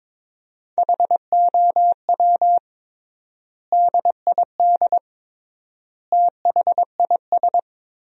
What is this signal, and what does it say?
Morse code, heard as a keyed tone, about 22 wpm, 700 Hz. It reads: HOW DID THIS
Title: 3x Word-Spacing - Sentences from Top 400 Words - 22wpm